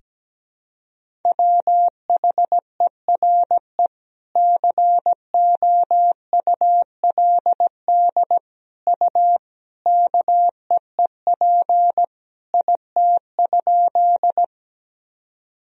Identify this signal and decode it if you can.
Morse code, heard as a keyed tone, about 17 wpm, 700 Hz. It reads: WHERE COULD U KEEP IT?
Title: Sentences from Top 300 Words and Abbreviations 17wpm